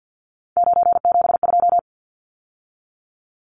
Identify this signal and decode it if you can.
Morse code, heard as a keyed tone, about 50 wpm, 700 Hz. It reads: 972
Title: Numbers - 3-Digit 50wpm